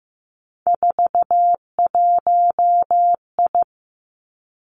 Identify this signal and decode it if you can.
Morse code, heard as a keyed tone, about 15 wpm, 700 Hz. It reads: 41I
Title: Three Challenging Letter-Number Combinations 15wpm